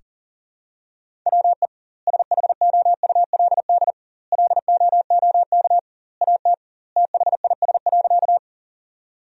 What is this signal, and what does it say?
Morse code, heard as a keyed tone, about 40 wpm, 700 Hz. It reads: WE SHOULD LOOK AT THIS.
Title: 2x Word-Spacing - Sentences from Top 300 Words - 40wpm